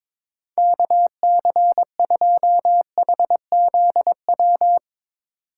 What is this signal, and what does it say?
Morse code, heard as a keyed tone, about 22 wpm, 700 Hz. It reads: KC2HZW
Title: USA Call Signs - 6 Characters - 6x Character Spacing - 22wpm